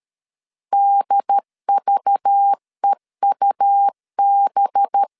Morse code, heard as a keyed tone, about 13 wpm, 800 Hz. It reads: DVEUB
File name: LEZIONI / 6 PKX0 / 6 PKX0 12wpm 2231